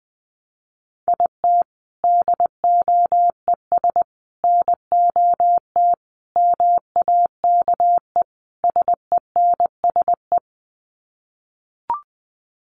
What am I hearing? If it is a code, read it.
Morse code, heard as a keyed tone, about 20 wpm, 700 Hz. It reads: IT DOES NOT MAKE SENSE
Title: Sets of 5 Words Encore2 20wpm